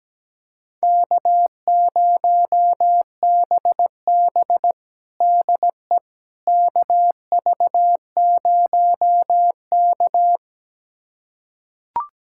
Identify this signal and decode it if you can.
Morse code, heard as a keyed tone, about 17 wpm, 700 Hz. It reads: K0BB DE KV0K
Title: USA Callsign DE Callsign - 4 Characters - 17wpm